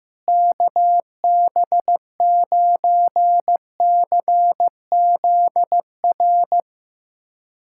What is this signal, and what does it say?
Morse code, heard as a keyed tone, about 15 wpm, 700 Hz. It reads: KB9CZR
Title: USA Callsign DE Callsign - 6 Characters - 15wpm